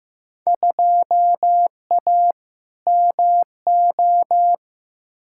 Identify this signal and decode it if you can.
Morse code, heard as a keyed tone, about 15 wpm, 700 Hz. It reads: 2A MO
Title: Field Day Exchanges Encore - 15wpm